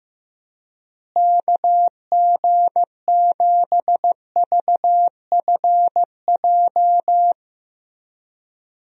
Morse code, heard as a keyed tone, about 15 wpm, 700 Hz. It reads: KG7VFJ
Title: USA Call Signs - 6 Characters - Repeated 3x - 15wpm